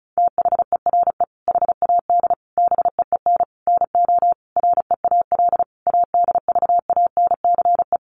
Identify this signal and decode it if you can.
Morse code, heard as a keyed tone, about 34 wpm, 700 Hz. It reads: THERE HAD BEEN NO REAL ADVANCE
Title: Sentences from Top 1400 Words 35wpm